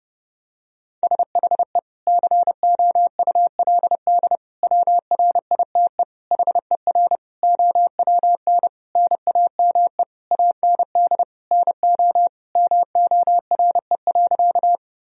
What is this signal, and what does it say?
Morse code, heard as a keyed tone, about 30 wpm, 700 Hz. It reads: SHE COULD WRITE HER OWN NAME AND NO MORE.
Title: Sentences from Top 1000 Words - Review Part 1 - 30wpm